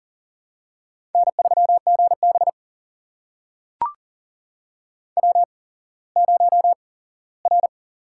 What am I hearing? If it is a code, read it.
Morse code, heard as a keyed tone, about 37 wpm, spaced as for 19 wpm, 700 Hz. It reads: N3GB W0R
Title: USA Call Signs - 4 Characters - 8x Character Spacing - 40wpm